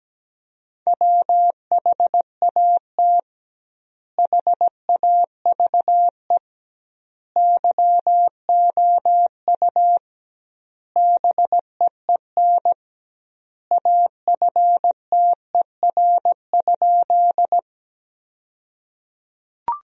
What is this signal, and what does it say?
Morse code, heard as a keyed tone, about 17 wpm, 700 Hz. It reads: WHAT HAVE YOU BEEN AFTER?
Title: 2x Word-Spacing - Sentences from Top 200 Words - 17wpm